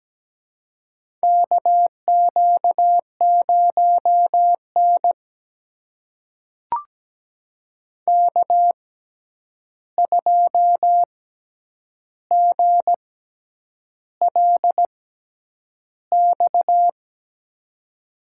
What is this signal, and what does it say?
Morse code, heard as a keyed tone, about 17 wpm, spaced as for 8 wpm, 700 Hz. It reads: KQ0N K2GLX
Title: USA Call Signs - 6x Character Spacing - 17wpm